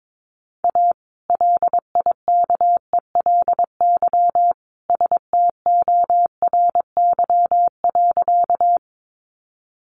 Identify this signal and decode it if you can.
Morse code, heard as a keyed tone, about 22 wpm, 700 Hz. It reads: A LIKELY STORY.
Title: Sentences from Top 600 Words 22wpm